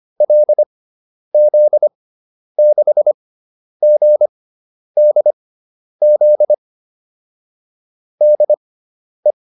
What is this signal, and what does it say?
Morse code, heard as a keyed tone, about 25 wpm, spaced as for 10 wpm, 600 Hz. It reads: LZ6GDZ DE